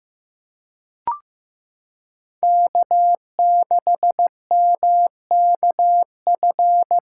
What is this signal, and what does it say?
Morse code, heard as a keyed tone, about 15 wpm, 700 Hz. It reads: K6MKF
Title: K1USN SST QSO Practice - 15wpm